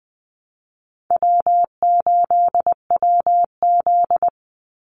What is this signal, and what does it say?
Morse code, heard as a keyed tone, about 20 wpm, 700 Hz. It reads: W8WZ
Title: Call Signs - 20wpm